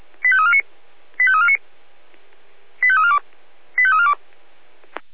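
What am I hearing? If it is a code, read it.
Five-tone selective call (ZVEI), heard as quick runs of five tones, tones of about 70 ms. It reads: 85349 85349 85342 85342
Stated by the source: ZVEI five-tone call received by SDR.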